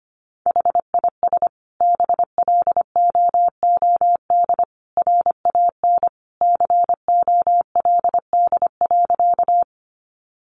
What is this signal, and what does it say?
Morse code, heard as a keyed tone, about 25 wpm, 700 Hz. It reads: HIS BLOOD RAN COLD.
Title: Sentences from Top 1100 Words 25wpm